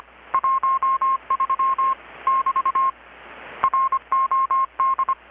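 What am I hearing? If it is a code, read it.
Morse code, heard as a keyed tone, about 25 wpm, 1070 Hz. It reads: 13 <BT> ROD
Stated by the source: Morse from KPH's Night of Nights XX maritime radiotelegraphy event, received in Chiapas, Mexico about 3,800 km away on a Kenwood R-600 with a grounded 10 m long-wire antenna.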